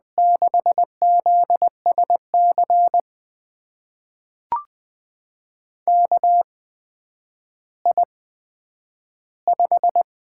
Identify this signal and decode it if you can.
Morse code, heard as a keyed tone, about 20 wpm, spaced as for 10 wpm, 700 Hz. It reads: 6ZSC KI5
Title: USA Call Signs - 6 Characters - 8x Character Spacing - 20wpm